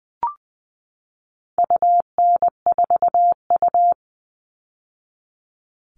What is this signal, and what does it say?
Morse code, heard as a keyed tone, about 20 wpm, 700 Hz. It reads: UN4U
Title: All Cut Numbers - 4-Digit 20wpm